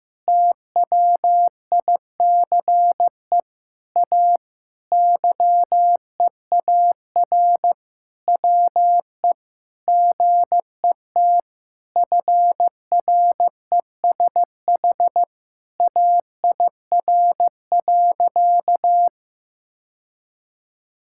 Morse code, heard as a keyed tone, about 15 wpm, 700 Hz. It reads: TWICE A YEAR WE GET FRESH AIR.
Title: Sentences from Top 1600 Words 15wpm